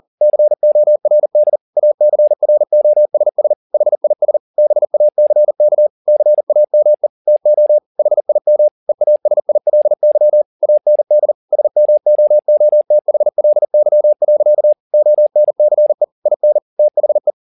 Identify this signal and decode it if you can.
Morse code, heard as a keyed tone, about 40 wpm, 600 Hz. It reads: CORD ACROSS HIS BACK CAME TO HIM EASILY AND SMOOTHLY. ONCE IN THE